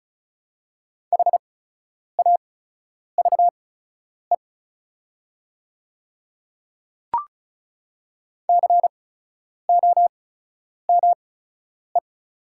Morse code, heard as a keyed tone, about 36 wpm, spaced as for 10 wpm, 690 Hz. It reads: HAVE COME
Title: 8x Character Spacing - Top 200 Words - 35wpm